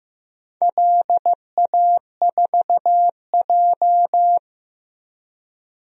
Morse code, heard as a keyed tone, about 15 wpm, 700 Hz. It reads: LA4J
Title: DXCC-100 Call Signs - 4 Characters - 15wpm